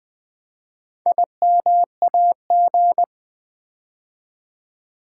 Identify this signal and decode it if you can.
Morse code, heard as a keyed tone, about 20 wpm, 700 Hz. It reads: IMAG
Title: Build A Word - 4 Letters At A Time - 11-Letter Words - 20wpm